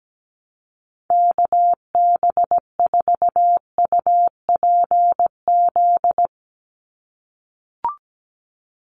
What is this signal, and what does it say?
Morse code, heard as a keyed tone, about 17 wpm, 700 Hz. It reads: KB4UPZ
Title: USA Call Signs 17wpm